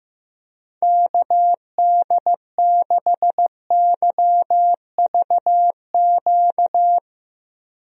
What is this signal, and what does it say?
Morse code, heard as a keyed tone, about 15 wpm, 700 Hz. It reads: KD6YVQ
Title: USA Call Signs - 6 Characters - 15wpm